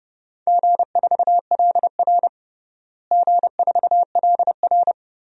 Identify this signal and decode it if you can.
Morse code, heard as a keyed tone, about 30 wpm, 700 Hz. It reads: G4LR G4LR
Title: DXCC-100 Call Signs - 4 Characters - Repeated 2x - 30wpm